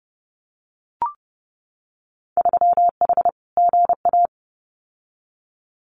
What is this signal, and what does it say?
Morse code, heard as a keyed tone, about 30 wpm, 700 Hz. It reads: 3H GA
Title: Winter Field Day Exchanges Encore 30wpm